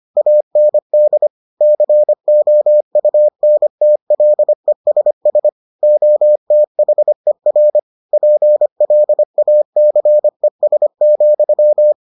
Morse code, heard as a keyed tone, about 25 wpm, 600 Hz. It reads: AND COUNTLESS OTHER PLACES,